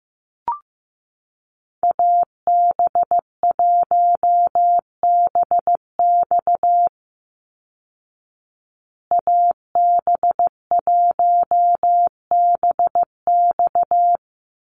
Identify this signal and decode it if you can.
Morse code, heard as a keyed tone, about 15 wpm, 700 Hz. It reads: AB1BX AB1BX
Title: Call Signs Repeated 3x - 15wpm